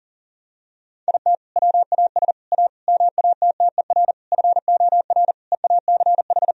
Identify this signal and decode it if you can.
Morse code, heard as a keyed tone, about 40 wpm, 700 Hz. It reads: IT WAS A MATTER FOR EACH
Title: Sentences from Top 1000 Words - Review Part 4 - 40wpm